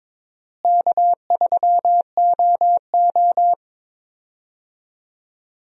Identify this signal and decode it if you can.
Morse code, heard as a keyed tone, about 22 wpm, 700 Hz. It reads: K3OO
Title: USA Call Signs - 4 Characters - Repeated 3x - 22wpm